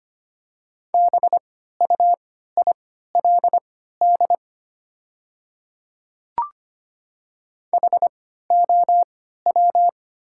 Morse code, heard as a keyed tone, about 25 wpm, spaced as for 14 wpm, 700 Hz. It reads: BUILD HOW
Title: 3x Character Spacing - Top 200 Words - 25wpm